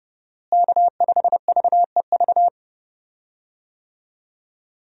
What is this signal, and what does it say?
Morse code, heard as a keyed tone, about 30 wpm, 700 Hz. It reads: K5VEV
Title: USA Call Signs - 5 Characters - Repeated 3x - 30wpm